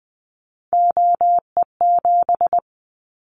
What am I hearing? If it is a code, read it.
Morse code, heard as a keyed tone, about 20 wpm, 700 Hz. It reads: OE7
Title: Callsign Prefixes for Top 25 DXCC - Repeated 3x - 20wpm